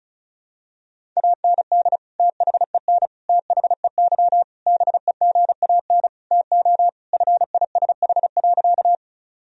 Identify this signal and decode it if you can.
Morse code, heard as a keyed tone, about 36 wpm, 700 Hz. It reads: AND THEN THEY BEGAN TO FISH.
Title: Sentences from Top 1000 Words - Review Part 1 - 35wpm